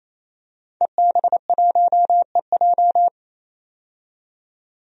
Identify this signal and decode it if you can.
Morse code, heard as a keyed tone, about 28 wpm, 700 Hz. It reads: EB1EJ
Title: DXCC-100 Call Signs - 5 Characters - 28wpm